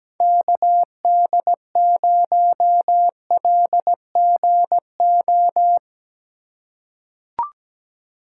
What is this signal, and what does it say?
Morse code, heard as a keyed tone, about 17 wpm, 700 Hz. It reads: KD0LGO
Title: USA Call Signs - 4x Character Spacing - 17wpm